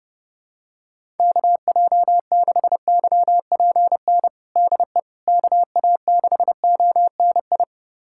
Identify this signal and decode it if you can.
Morse code, heard as a keyed tone, about 30 wpm, 700 Hz. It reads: KJ6YPN DE KA6ONI
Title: USA Callsign DE Callsign - 6 Characters - 30wpm